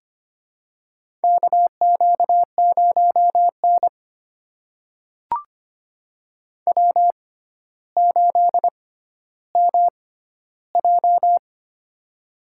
Morse code, heard as a keyed tone, about 25 wpm, spaced as for 13 wpm, 700 Hz. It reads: KQ0N W8MJ
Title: USA Call Signs - 4 Characters - 6x Character Spacing - 25wpm